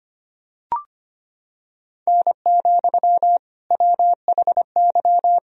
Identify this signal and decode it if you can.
Morse code, heard as a keyed tone, about 25 wpm, 700 Hz. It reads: N, WHY
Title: Sentences from Top 300 Words and Abbreviations 25wpm